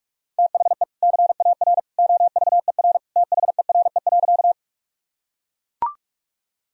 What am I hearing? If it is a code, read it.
Morse code, heard as a keyed tone, about 41 wpm, 700 Hz. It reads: THE CAR OVER THERE.
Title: Sentences from Top 1000 Words - Review Part 1 - 45wpm